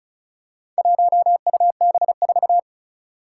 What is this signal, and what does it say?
Morse code, heard as a keyed tone, about 36 wpm, 700 Hz. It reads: 1UB4
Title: Four Challenging Letter-Number Combinations 35wpm